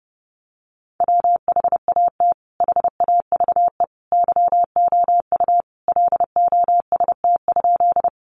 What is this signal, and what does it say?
Morse code, heard as a keyed tone, about 28 wpm, 700 Hz. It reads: WHAT HAVE YOU LOST?
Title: Sentences from Top 1400 Words 30wpm